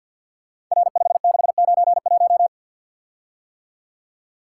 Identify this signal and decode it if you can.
Morse code, heard as a keyed tone, about 51 wpm, 700 Hz. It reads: R5691
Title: Code Groups - 5 Letter:Numbers - 50wpm